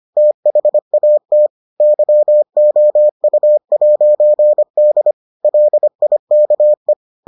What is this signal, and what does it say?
Morse code, heard as a keyed tone, about 25 wpm, 600 Hz. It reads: THAT YOU'D LIKE